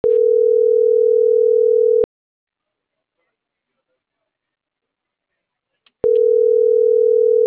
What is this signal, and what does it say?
Ringback tone (North America), 440 + 480 Hz.